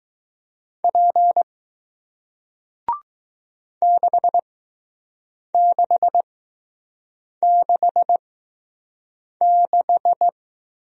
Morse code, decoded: P 6666

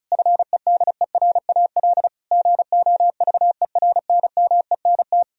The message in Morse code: FEDERAL GOVERNMENT

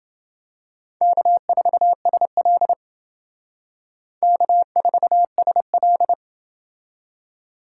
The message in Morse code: K4SL K4SL

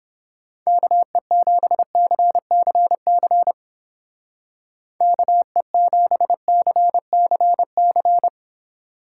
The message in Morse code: KE7CCC KE7CCC

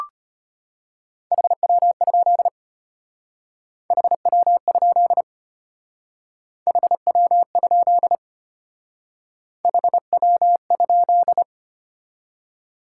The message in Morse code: HW? HW? HW? HW?

